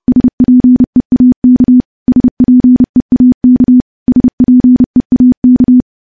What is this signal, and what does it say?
Morse code, heard as a keyed tone, about 28 wpm, 250 Hz. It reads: SPEAK SPEAK SPEAK
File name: morse / morse-training-30WPM